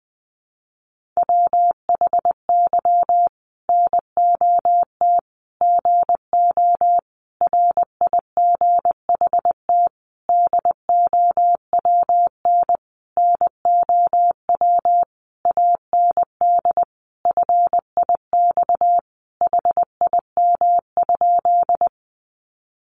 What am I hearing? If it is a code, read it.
Morse code, heard as a keyed tone, about 20 wpm, 700 Hz. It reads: WHY NOT GO RIGHT DOWN NOW AND FIX HIM?
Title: Sentences from Top 1400 Words 20wpm